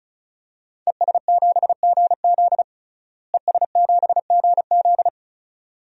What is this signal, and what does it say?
Morse code, heard as a keyed tone, about 36 wpm, 700 Hz. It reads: ES7GZ ES7GZ
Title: DXCC-100 Call Signs - 5 Characters - Repeated 2x - 35wpm